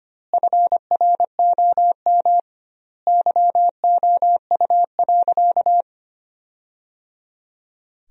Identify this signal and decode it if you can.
Morse code, heard as a keyed tone, about 25 wpm, 700 Hz. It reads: FROM YOU.
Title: 2x Word-Spacing - Sentences from Top 100 Words - 25wpm